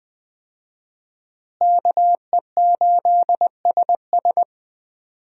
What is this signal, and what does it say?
Morse code, heard as a keyed tone, about 21 wpm, 700 Hz. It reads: KE8SS
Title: USA Call Signs - 5 Characters - Repeated 2x - 20wpm